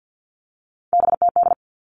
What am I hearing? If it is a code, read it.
Morse code, heard as a keyed tone, about 50 wpm, 700 Hz. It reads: BTD
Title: All Cut Numbers - 3-Digit 50wpm